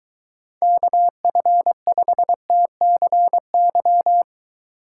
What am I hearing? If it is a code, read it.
Morse code, heard as a keyed tone, about 23 wpm, 700 Hz. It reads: KF5TCY